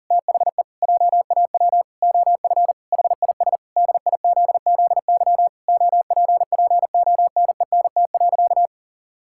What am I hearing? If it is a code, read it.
Morse code, heard as a keyed tone, about 37 wpm, 700 Hz. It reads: THE JAW OF HIS DIZZY OPPONENT.
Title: Pangram Challenge v2 - 40wpm